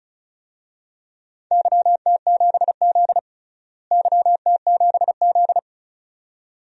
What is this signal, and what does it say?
Morse code, heard as a keyed tone, about 36 wpm, 700 Hz. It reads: YT7Z YT7Z
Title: DXCC-100 Call Signs - 4 Characters - Repeated 2x - 35wpm